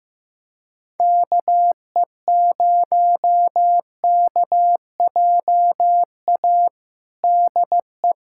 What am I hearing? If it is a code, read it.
Morse code, heard as a keyed tone, about 16 wpm, 700 Hz. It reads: KE0KJA DE
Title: USA Callsign DE Callsign - 6 Characters - 15wpm